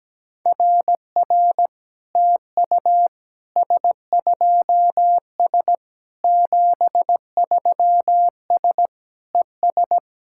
Morse code, heard as a keyed tone, about 17 wpm, 700 Hz. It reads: RR TU S2S 73S ES
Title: SOTA QSO Practice 17wpm